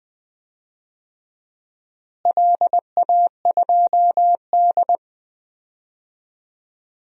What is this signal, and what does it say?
Morse code, heard as a keyed tone, about 20 wpm, 700 Hz. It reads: LA2D
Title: DXCC-100 Call Signs - 4 Characters - Repeated 3x - 20wpm